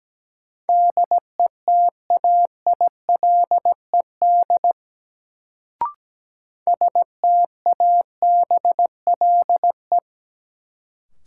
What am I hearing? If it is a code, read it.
Morse code, heard as a keyed tone, about 17 wpm, 700 Hz. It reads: DETAILED STABLE
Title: Sentences from Top 1500 Words - New Word Drill - 17wpm